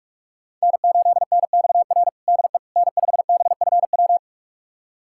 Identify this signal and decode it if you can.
Morse code, heard as a keyed tone, about 45 wpm, 700 Hz. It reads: N8NXR DE N5BFW
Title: USA Callsign DE Callsign - 5 Characters - 45wpm